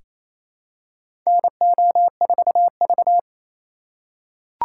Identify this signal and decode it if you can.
Morse code, heard as a keyed tone, about 28 wpm, 700 Hz. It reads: NO4V
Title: USA Call Signs - 4 Characters - 3x Character Spacing - 28wpm